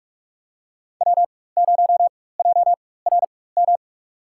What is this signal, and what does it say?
Morse code, heard as a keyed tone, about 44 wpm, spaced as for 21 wpm, 690 Hz. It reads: W0JRM
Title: USA Call Signs - 4x Character Spacing - 45wpm